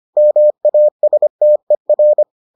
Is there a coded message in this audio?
Yes: Morse code from 0.2 to 2.2 s, heard as a keyed tone, about 25 wpm, 600 Hz. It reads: MASTER